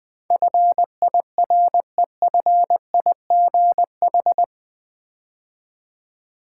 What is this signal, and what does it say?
Morse code, heard as a keyed tone, about 20 wpm, 700 Hz. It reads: FIREFIGH
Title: Build A Word - 4 Letters At A Time - 11-Letter Words - 20wpm